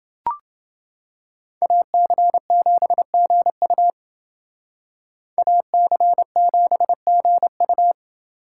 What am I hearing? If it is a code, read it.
Morse code, heard as a keyed tone, about 28 wpm, 700 Hz. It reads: AC7GU AC7GU